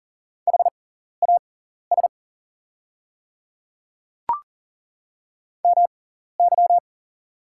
Morse code, heard as a keyed tone, about 40 wpm, spaced as for 14 wpm, 700 Hz. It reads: HAS MY
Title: 6x Character Spacing - Top 300 Words - 40wpm